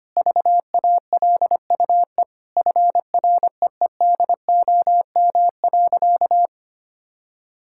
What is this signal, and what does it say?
Morse code, heard as a keyed tone, about 25 wpm, 700 Hz. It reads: VALUE FREEDOM.
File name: practice / 25wpm / Sentences with 3-Word Phrases - 25wpm